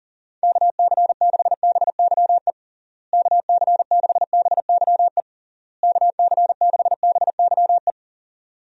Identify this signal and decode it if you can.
Morse code, heard as a keyed tone, about 40 wpm, 700 Hz. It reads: KC6BYE KC6BYE KC6BYE
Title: USA Call Signs - 6 Characters - Repeated 3x - 40wpm